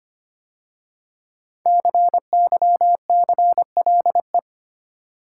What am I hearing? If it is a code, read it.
Morse code, heard as a keyed tone, about 25 wpm, 700 Hz. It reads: CYCLE
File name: 5 Letter Words - First 3 Letters Given - 25wpm